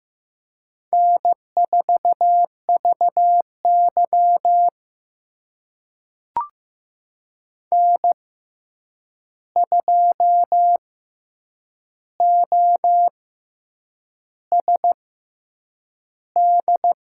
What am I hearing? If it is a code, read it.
Morse code, heard as a keyed tone, about 15 wpm, spaced as for 7 wpm, 700 Hz. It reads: N4VY N2OSD
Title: USA Call Signs - 6x Character Spacing - 15wpm